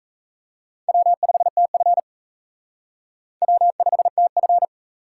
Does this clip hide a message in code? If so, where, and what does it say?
Yes: Morse code from 0.9 to 4.7 s, heard as a keyed tone, about 37 wpm, 700 Hz. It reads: W5TF W5TF